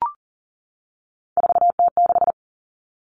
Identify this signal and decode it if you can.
Morse code, heard as a keyed tone, about 40 wpm, 700 Hz. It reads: 4T6
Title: All Cut Numbers - 3-Digit 40wpm